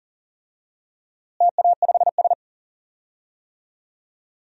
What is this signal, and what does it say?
Morse code, heard as a keyed tone, about 40 wpm, 700 Hz. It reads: TA5S
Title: DXCC-100 Call Signs - 4 Characters - Repeated 2x - 40wpm